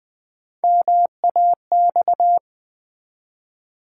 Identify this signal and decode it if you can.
Morse code, heard as a keyed tone, about 20 wpm, 700 Hz. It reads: MAX
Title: Build A Word - 3 Letters At A Time - 8-Letter Words - 20wpm